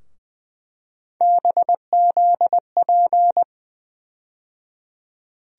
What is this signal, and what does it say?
Morse code, heard as a keyed tone, about 21 wpm, 700 Hz. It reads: BZP
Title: USA Call Sign Postfixes - 3 Letters 20wpm